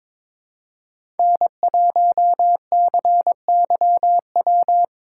Morse code, heard as a keyed tone, about 22 wpm, 700 Hz. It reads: N1CYW
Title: USA Call Signs - 5 Characters - 2x Character Spacing - 22wpm